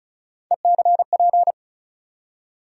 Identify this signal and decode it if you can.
Morse code, heard as a keyed tone, about 36 wpm, 700 Hz. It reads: ECP
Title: USA Call Sign Postfixes - 3 Letters 35wpm